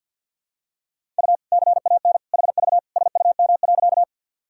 Morse code, heard as a keyed tone, about 50 wpm, 700 Hz. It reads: U CAN HV SUM.
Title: Sentences from Top 100 Words and Abbreviations 50wpm